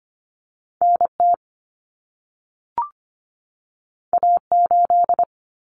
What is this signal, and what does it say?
Morse code, heard as a keyed tone, about 25 wpm, 700 Hz. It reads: NT A8
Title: Common Cut Numbers - 2-Digit 25wpm